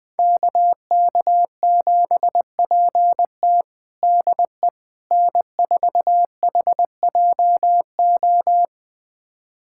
Morse code, heard as a keyed tone, about 21 wpm, 700 Hz. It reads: KK7PT DE N4HJO
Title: USA Callsign DE Callsign - 5 Characters - 20wpm